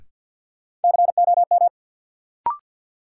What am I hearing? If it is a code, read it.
Morse code, heard as a keyed tone, about 53 wpm, 700 Hz. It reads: COM
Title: Sentences from Top 1300 Words - New Word Drill Alt - 50wpm